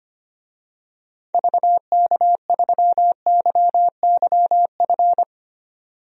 Morse code, heard as a keyed tone, about 25 wpm, 700 Hz. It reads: VK3YYF
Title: DXCC-100 Call Signs - 6 Characters - 25wpm